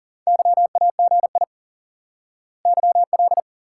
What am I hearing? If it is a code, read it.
Morse code, heard as a keyed tone, about 40 wpm, 700 Hz. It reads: YAGI YL
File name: IQE Course 3 - Lesson 89 - YAGI - Two Elements - 40wpm